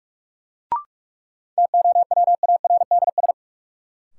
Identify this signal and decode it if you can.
Morse code, heard as a keyed tone, about 46 wpm, 700 Hz. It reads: TOWARDS
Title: Sentences from Top 1000 Words - Review Part 2 - New Word Drill - 45wpm